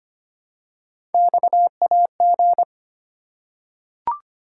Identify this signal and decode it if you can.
Morse code, heard as a keyed tone, about 25 wpm, 700 Hz. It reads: XAG
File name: Three Letter-Number Combinations - 25wpm